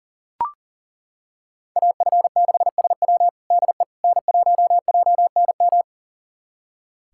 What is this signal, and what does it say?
Morse code, heard as a keyed tone, about 40 wpm, 700 Hz. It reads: AF6SW DE N1JNM
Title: USA Callsign DE Callsign - 5 Characters - 40wpm